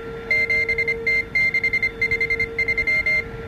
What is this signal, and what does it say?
Morse code, heard as a keyed tone, about 25 wpm, 2100 Hz, with noise. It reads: 7T653